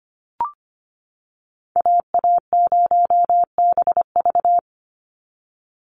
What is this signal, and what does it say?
Morse code, heard as a keyed tone, about 25 wpm, 700 Hz. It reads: AA0BV
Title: NA SOTA Activator Callsigns 25wpm